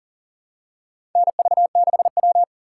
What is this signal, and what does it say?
Morse code, heard as a keyed tone, about 40 wpm, 700 Hz. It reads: NV6W
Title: USA Call Signs - 4 Characters - 4x Character Spacing - 40wpm